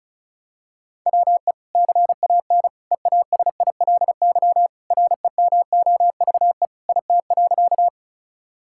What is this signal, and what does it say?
Morse code, heard as a keyed tone, about 36 wpm, 700 Hz. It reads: WE CAN EASILY REMOVE IT.
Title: Sentences from Top 1000 Words - Review Part 5 - 35wpm